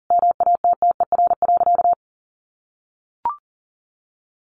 Morse code, heard as a keyed tone, about 40 wpm, 700 Hz. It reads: MATTER.
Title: Sentences from Top 1200 Words 40wpm